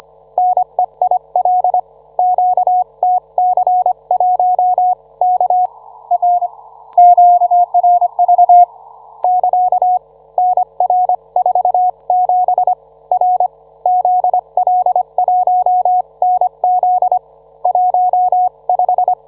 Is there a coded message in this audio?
Yes: Morse code from 0.4 to 19.1 s, heard as a keyed tone, about 25 wpm, 730 Hz. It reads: NEIL QTC1 K R QRV <KA> NR47 R ZL1NZ 15